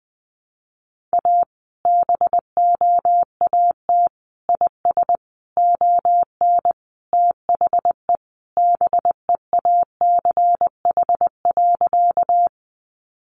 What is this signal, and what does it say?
Morse code, heard as a keyed tone, about 20 wpm, 700 Hz. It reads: A BOAT IS ON THE BEACH.
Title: Sentences from Top 1500 Words 20wpm